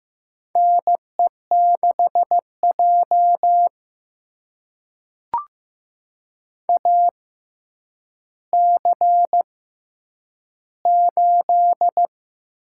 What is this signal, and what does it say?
Morse code, heard as a keyed tone, about 15 wpm, spaced as for 9 wpm, 700 Hz. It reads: NE6J AC8
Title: USA Call Signs - 6x Character Spacing - 15wpm